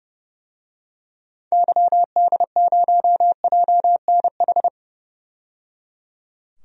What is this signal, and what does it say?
Morse code, heard as a keyed tone, about 30 wpm, 700 Hz. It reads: YD0JNH
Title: DXCC-100 Call Signs - 6 Characters - 30wpm